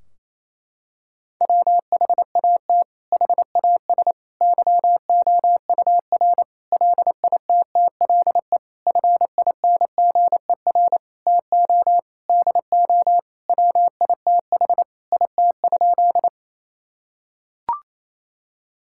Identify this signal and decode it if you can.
Morse code, heard as a keyed tone, about 28 wpm, 700 Hz. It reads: WHAT HAS YOUR LITTLE FINGER TO DO WITH IT?
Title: Sentences from Top 1600 Words 28wpm